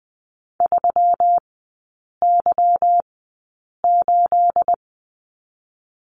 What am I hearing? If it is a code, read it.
Morse code, heard as a keyed tone, about 19 wpm, 700 Hz. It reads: 3 Y 8
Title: 212 Lesson 34e - Number 8 - Three Characters 20wpm